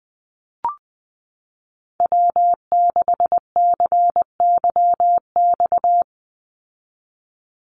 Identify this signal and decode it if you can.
Morse code, heard as a keyed tone, about 20 wpm, 700 Hz. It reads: W6CYX
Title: Call Signs Repeated 3x - 20wpm